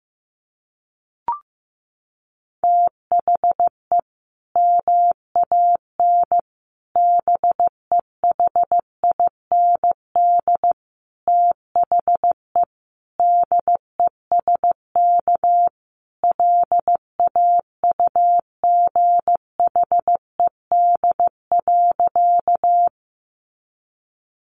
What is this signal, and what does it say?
Morse code, heard as a keyed tone, about 15 wpm, 700 Hz. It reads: THE MAN BEHIND THE DESK LAUGHED.
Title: A Sound Of Thunder 15wpm